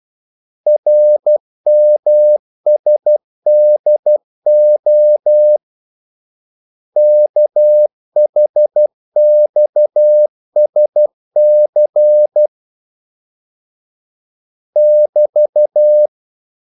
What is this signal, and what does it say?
Morse code, heard as a keyed tone, about 12 wpm, 600 Hz. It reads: RMSDO KHXSC <BT>
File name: code-exemple